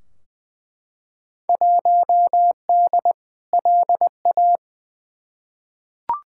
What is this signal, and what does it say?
Morse code, heard as a keyed tone, about 20 wpm, 700 Hz. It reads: 1D LA
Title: Field Day Exchanges - 20wpm